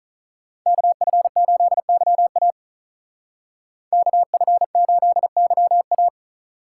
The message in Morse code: KF8YA KF8YA

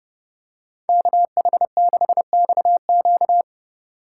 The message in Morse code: KH6XQ

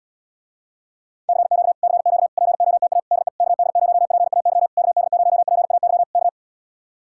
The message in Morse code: CQ CQ POTA DE KD0PAY KD0PAY K

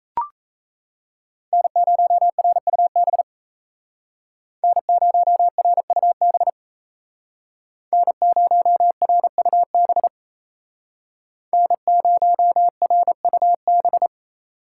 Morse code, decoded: N0RUB N0RUB N0RUB N0RUB

